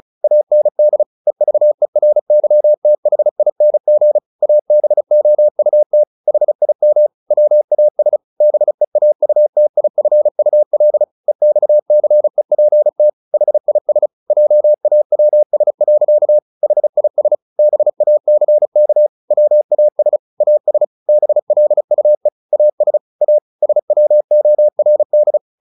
Morse code, heard as a keyed tone, about 34 wpm, 600 Hz. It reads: AND EVERYTHING ABOUT HIM WAS BEAUTIFUL EXCEPT HIS JAWS. HIS BACK WAS AS BLUE AS A SWORD
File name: the old man and the sea 2 35wpm